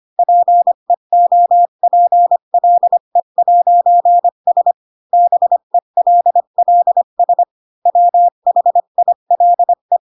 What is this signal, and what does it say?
Morse code, heard as a keyed tone, about 25 wpm, 700 Hz. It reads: PEOPLE'S BELLS WHILE